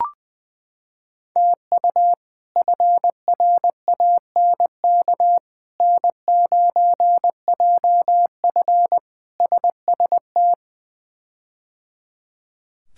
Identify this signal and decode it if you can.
Morse code, heard as a keyed tone, about 20 wpm, 700 Hz. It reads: TU FRANK N9JF SST
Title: K1USN SST QSO Practice - 20wpm